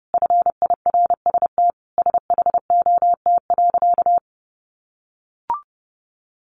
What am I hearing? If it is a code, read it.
Morse code, heard as a keyed tone, about 30 wpm, 700 Hz. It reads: FIRST SHOT.
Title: A Sound Of Thunder 30wpm